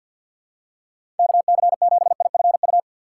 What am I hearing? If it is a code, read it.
Morse code, heard as a keyed tone, about 53 wpm, 700 Hz. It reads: KC7IFU
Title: USA Call Signs - 6 Characters - 4x Character Spacing - 50wpm